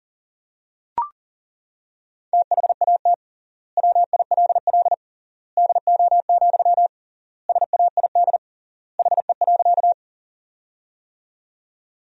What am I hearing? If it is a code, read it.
Morse code, heard as a keyed tone, about 40 wpm, 700 Hz. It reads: THAT WILL DO, SAID HE.
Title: 3x Word-Spacing - Sentences from Top 100 Words - 40wpm